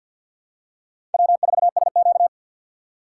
Morse code, heard as a keyed tone, about 53 wpm, 700 Hz. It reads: W4SQ
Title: USA Call Signs - 4 Characters - 3x Character Spacing - 50wpm